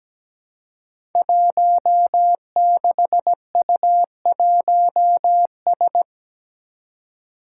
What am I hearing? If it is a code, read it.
Morse code, heard as a keyed tone, about 17 wpm, 700 Hz. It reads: 16U1S
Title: Five Challenging Letter-Number Combinations 17wpm